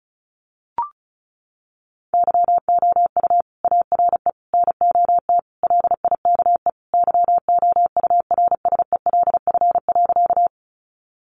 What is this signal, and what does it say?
Morse code, heard as a keyed tone, about 34 wpm, 700 Hz. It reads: YOU ARE NOT LIKE YOURSELF.
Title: Sentences from Top 600 Words 35wpm